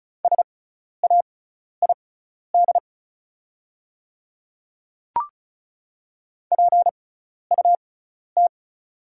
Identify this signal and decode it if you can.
Morse code, heard as a keyed tone, about 34 wpm, spaced as for 12 wpm, 700 Hz. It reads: SAID PUT
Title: 6x Character Spacing - Top 200 Words - 35wpm